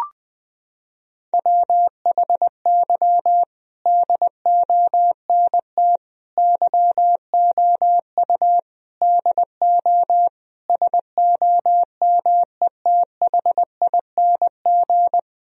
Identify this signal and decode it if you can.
Morse code, heard as a keyed tone, about 20 wpm, 700 Hz. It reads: WHY DONT YOU DO SOMETHING
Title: Sentences from Top 1000 Words - Review Part 4 - 20wpm